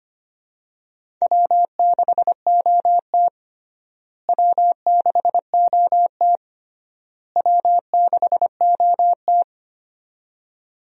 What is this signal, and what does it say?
Morse code, heard as a keyed tone, about 25 wpm, 700 Hz. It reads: W6OT W6OT W6OT
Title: USA Call Signs - 4 Characters - Repeated 3x - 25wpm